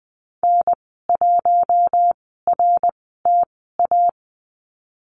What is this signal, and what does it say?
Morse code, heard as a keyed tone, about 20 wpm, spaced as for 15 wpm, 700 Hz. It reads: N1RTA
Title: 055 Lesson 10g - Letter r - Call Signs 20wpm